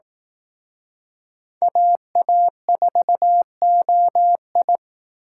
Morse code, heard as a keyed tone, about 18 wpm, 700 Hz. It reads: AA4OI
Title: Speed Racing - USA 2x2 Call Signs 15wpm